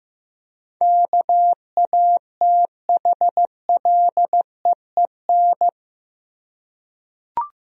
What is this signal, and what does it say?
Morse code, heard as a keyed tone, about 16 wpm, 700 Hz. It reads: KATHLEEN
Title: USA Common Names - 15wpm